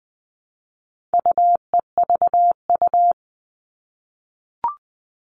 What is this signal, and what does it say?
Morse code, heard as a keyed tone, about 20 wpm, 700 Hz. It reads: UEVU
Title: All Cut Numbers - 4-Digit 20wpm